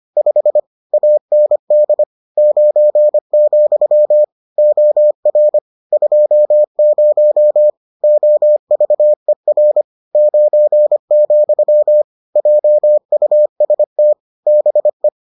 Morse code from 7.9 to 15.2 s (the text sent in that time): OVER 9, JUST BE